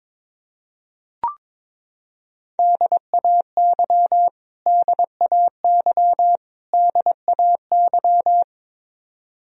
Morse code, heard as a keyed tone, about 22 wpm, 700 Hz. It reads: DAY DAY DAY
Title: Top 100 Words Repeated 22wpm